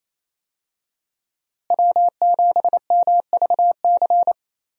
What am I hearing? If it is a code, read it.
Morse code, heard as a keyed tone, about 27 wpm, 700 Hz. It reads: W7MVC
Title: USA Call Signs - 5 Characters - Repeated 2x - 28wpm